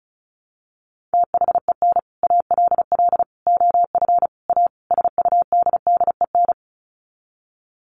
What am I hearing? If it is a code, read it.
Morse code, heard as a keyed tone, about 34 wpm, 700 Hz. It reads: THEN ALL OF A SUDDEN
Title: Sets of 5 Words Encore 35wpm